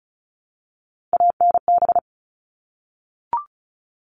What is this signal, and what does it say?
Morse code, heard as a keyed tone, about 34 wpm, 700 Hz. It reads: ANB
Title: All Cut Numbers - 3-Digit 35wpm